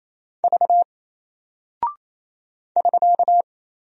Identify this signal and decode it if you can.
Morse code, heard as a keyed tone, about 27 wpm, 700 Hz. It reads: V <SK>
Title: Single Character (Letter:Number:Punctuation:Prosign) Alt 28wpm